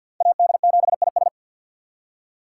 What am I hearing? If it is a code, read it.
Morse code, heard as a keyed tone, about 59 wpm, 700 Hz. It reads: AD7IS
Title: Speed Racing - USA 2x2 Call Signs 50wpm